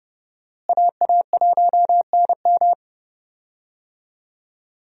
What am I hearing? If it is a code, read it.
Morse code, heard as a keyed tone, about 31 wpm, 700 Hz. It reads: AA1NM
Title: USA Call Signs - 5 Characters - Repeated 3x - 30wpm